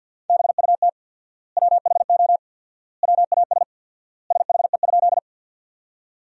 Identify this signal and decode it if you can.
Morse code, heard as a keyed tone, about 50 wpm, 700 Hz. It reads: BUT WHO WAS SHE?
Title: 4x Word-Spacing - Sentences from Top 100 Words - 50wpm